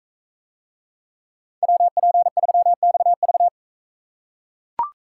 Morse code, heard as a keyed tone, about 40 wpm, 700 Hz. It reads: WP3XV